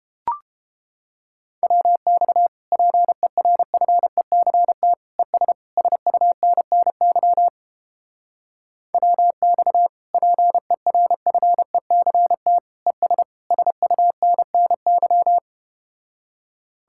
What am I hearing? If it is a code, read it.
Morse code, heard as a keyed tone, about 32 wpm, 700 Hz. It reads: WX PERFECT ES SUNNY WX PERFECT ES SUNNY